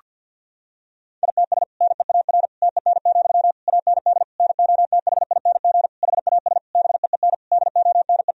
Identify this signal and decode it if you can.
Morse code, heard as a keyed tone, about 52 wpm, 700 Hz. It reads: ITS NEAR TEN, AND NOTHING HAS BEEN DONE
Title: Sentences from Top 1000 Words - Review Part 2 - 50wpm